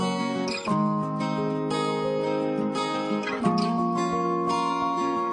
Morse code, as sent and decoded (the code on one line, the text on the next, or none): none